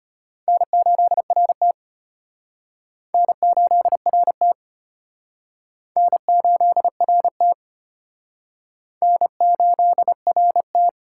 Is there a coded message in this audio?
Yes: Morse code from 0.5 to 10.9 s, heard as a keyed tone, about 32 wpm, 700 Hz. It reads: N8RT N8RT N8RT N8RT